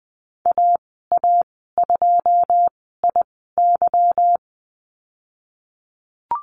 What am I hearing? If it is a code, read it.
Morse code, heard as a keyed tone, about 20 wpm, spaced as for 15 wpm, 700 Hz. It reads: AA2IY
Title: 154 Lesson 25g - Letter p - Call Signs 20wpm